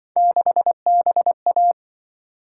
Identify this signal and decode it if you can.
Morse code, heard as a keyed tone, about 24 wpm, 700 Hz. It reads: 6BA
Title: Speed-Racing - Three Letter-Number Combinations - 20wpm